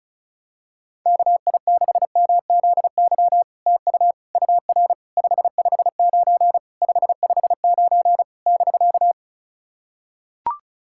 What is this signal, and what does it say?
Morse code, heard as a keyed tone, about 36 wpm, 700 Hz. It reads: KI6MZY TU UR 559 559 <BK>
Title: SOTA QSO Practice 35wpm